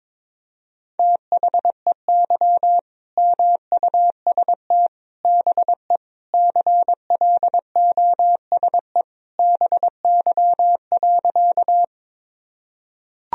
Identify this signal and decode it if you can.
Morse code, heard as a keyed tone, about 22 wpm, 700 Hz. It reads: THEY MUST BE CLOSE BY.
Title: Sentences from Top 1000 Words - Review Part 1 - 22wpm